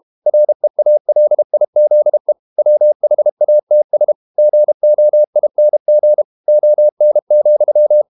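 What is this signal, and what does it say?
Morse code, heard as a keyed tone, about 32 wpm, 600 Hz. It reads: REALIZE WHATS GOING ON,
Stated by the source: Morse code practice at 32 wpm from CW med Gnister, reading Ken Follett's novel Never in English.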